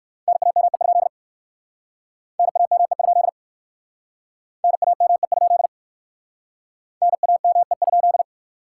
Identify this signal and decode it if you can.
Morse code, heard as a keyed tone, about 54 wpm, 700 Hz. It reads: NAME? NAME? NAME? NAME?
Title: Speed-Racing Common QSO Phrases - 45wpm